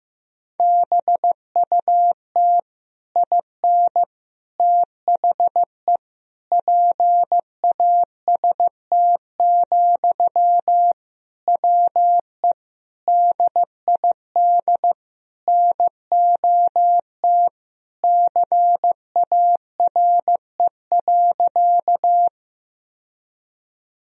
Morse code, heard as a keyed tone, about 15 wpm, 700 Hz. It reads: BUT IN THE PAST, WE DID NOT CARE.
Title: Sentences from Top 1000 Words - Review Part 3 - 15wpm